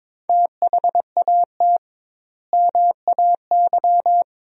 Morse code, heard as a keyed tone, about 22 wpm, 700 Hz. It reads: THAT MAY
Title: 2x Word-Spacing - Sentences from Top 400 Words - 22wpm